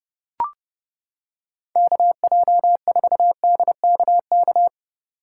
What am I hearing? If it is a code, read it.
Morse code, heard as a keyed tone, about 28 wpm, 700 Hz. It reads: KJ4DKK